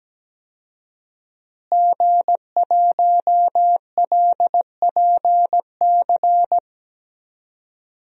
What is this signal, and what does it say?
Morse code, heard as a keyed tone, about 17 wpm, 700 Hz. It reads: G1LPC
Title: DXCC-100 Call Signs - 5 Characters - Repeated 2x - 17wpm